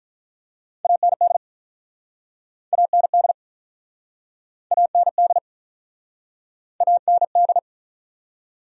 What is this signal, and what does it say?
Morse code, heard as a keyed tone, about 48 wpm, 700 Hz. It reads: AND AND AND AND